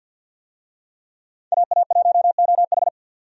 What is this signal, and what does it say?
Morse code, heard as a keyed tone, about 51 wpm, 700 Hz. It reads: AA1OH
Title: USA Call Signs - 5 Characters - 50wpm